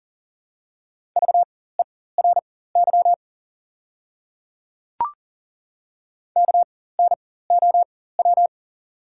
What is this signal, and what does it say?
Morse code, heard as a keyed tone, about 40 wpm, spaced as for 19 wpm, 690 Hz. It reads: VERY KNOW
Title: 4x Character Spacing - Top 300 Words - 40wpm